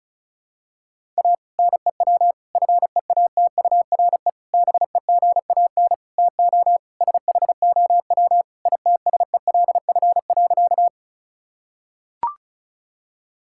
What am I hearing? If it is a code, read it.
Morse code, heard as a keyed tone, about 36 wpm, 700 Hz. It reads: A NEW FEATURE BEGAN TO SHOW ITSELF.
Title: Sentences from Top 1000 Words - Review Part 5 - 35wpm